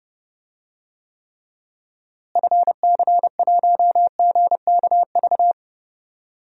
Morse code, heard as a keyed tone, about 29 wpm, 700 Hz. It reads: FC1GKV
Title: DXCC-100 Call Signs - 6 Characters - Repeated 2x - 30wpm